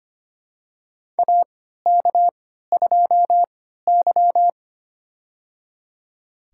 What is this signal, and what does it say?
Morse code, heard as a keyed tone, about 25 wpm, spaced as for 14 wpm, 700 Hz. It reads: AK2Y
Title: USA Call Signs - 4 Characters - 3x Character Spacing - 25wpm